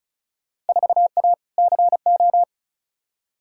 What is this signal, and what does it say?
Morse code, heard as a keyed tone, about 34 wpm, 700 Hz. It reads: 4A CO
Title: Field Day Exchanges - 35wpm